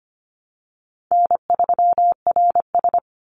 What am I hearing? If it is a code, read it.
Morse code, heard as a keyed tone, about 25 wpm, 700 Hz. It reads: N3RS
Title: Easy Call Signs Repeated 2x - 25wpm